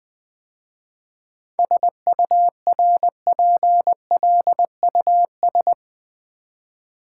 Morse code, heard as a keyed tone, about 20 wpm, 700 Hz. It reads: SURPLUS
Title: 7 Letter Words - First 3 Letters Given - 20wpm